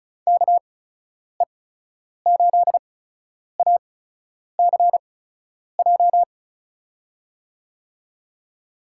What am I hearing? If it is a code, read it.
Morse code, heard as a keyed tone, about 36 wpm, spaced as for 10 wpm, 700 Hz. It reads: KE8ACJ
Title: USA Call Signs - 6 Characters - 8x Character Spacing - 35wpm